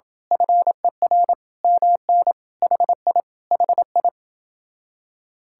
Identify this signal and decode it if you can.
Morse code, heard as a keyed tone, about 26 wpm, 700 Hz. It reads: FER MN HI HI